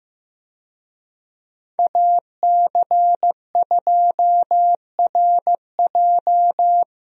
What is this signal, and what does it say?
Morse code, heard as a keyed tone, about 15 wpm, 700 Hz. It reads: AC2RJ
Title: USA Call Signs - 5 Characters - Repeated 3x - 15wpm